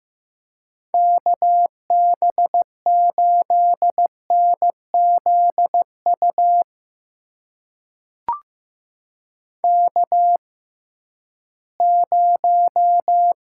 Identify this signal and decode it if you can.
Morse code, heard as a keyed tone, about 15 wpm, spaced as for 11 wpm, 700 Hz. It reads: KB8NZU K0
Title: USA Call Signs - 6x Character Spacing - 15wpm